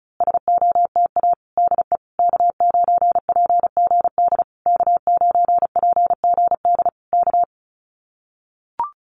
Morse code, heard as a keyed tone, about 34 wpm, 700 Hz. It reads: SOTA DE K9PGD K9PGD K
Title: Common QSO Phrases Encore 35wpm